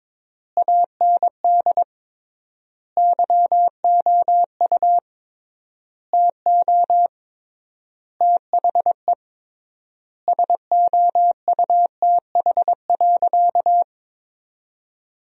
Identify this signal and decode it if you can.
Morse code, heard as a keyed tone, about 22 wpm, 700 Hz. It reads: AND YOU TO THE SOUTH.
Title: 3x Word-Spacing - Sentences from Top 400 Words - 22wpm